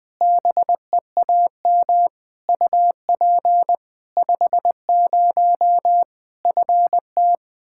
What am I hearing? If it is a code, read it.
Morse code, heard as a keyed tone, about 21 wpm, 700 Hz. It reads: BEAM UP 50 FT
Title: Practice QSO - Exchange 2 - Info - 20wpm